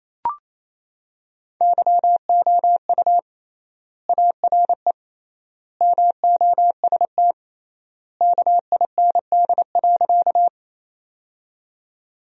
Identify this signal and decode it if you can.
Morse code, heard as a keyed tone, about 28 wpm, 700 Hz. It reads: YOU ARE MOST KIND.
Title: 3x Word-Spacing - Sentences from Top 200 Words - 28wpm